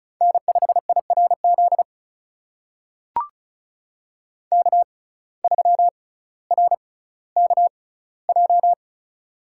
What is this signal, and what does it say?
Morse code, heard as a keyed tone, about 34 wpm, spaced as for 18 wpm, 700 Hz. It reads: N5IRZ K3RKJ
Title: USA Call Signs - 5 Characters - 6x Character Spacing - 35wpm